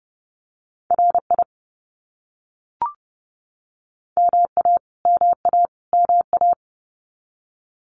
Morse code, heard as a keyed tone, about 30 wpm, 700 Hz. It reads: RI MA MA MA
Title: US States Repeated 3x 30wpm